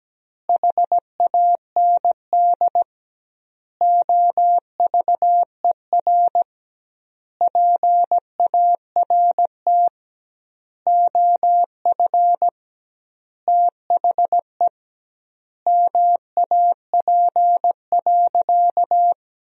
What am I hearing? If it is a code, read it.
Morse code, heard as a keyed tone, about 17 wpm, 700 Hz. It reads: HAND OVER PART OF THE MAP.
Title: 2x Word-Spacing - Sentences from Top 400 Words - 17wpm